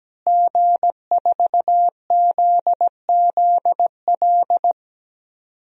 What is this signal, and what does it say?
Morse code, heard as a keyed tone, about 17 wpm, 700 Hz. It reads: G4ZZL
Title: DXCC-100 Call Signs - 5 Characters - Repeated 3x - 17wpm